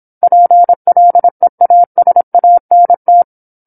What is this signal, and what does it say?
Morse code, heard as a keyed tone, about 25 wpm, 700 Hz. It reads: PLEASANT